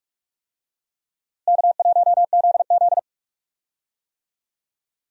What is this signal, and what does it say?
Morse code, heard as a keyed tone, about 46 wpm, 700 Hz. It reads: K1ZZ
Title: USA Call Signs - 4 Characters - Repeated 2x - 45wpm